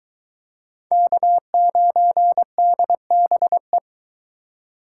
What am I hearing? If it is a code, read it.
Morse code, heard as a keyed tone, about 21 wpm, 700 Hz. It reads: K9DBE